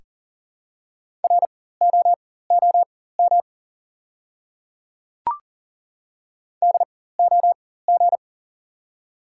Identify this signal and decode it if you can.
Morse code, heard as a keyed tone, about 40 wpm, spaced as for 19 wpm, 700 Hz. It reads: ROOM DOG
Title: 4x Character Spacing - Top 500 Words - 40wpm